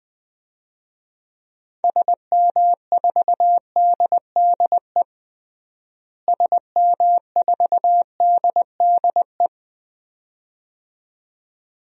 Morse code, heard as a keyed tone, about 20 wpm, 700 Hz. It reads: SM4DDE SM4DDE
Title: DXCC-100 Call Signs - 6 Characters - Repeated 2x - 20wpm